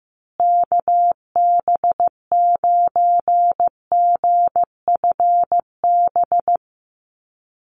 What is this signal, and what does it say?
Morse code, heard as a keyed tone, about 15 wpm, 700 Hz. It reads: KB9GFB
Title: USA Call Signs Repeated 2x 15wpm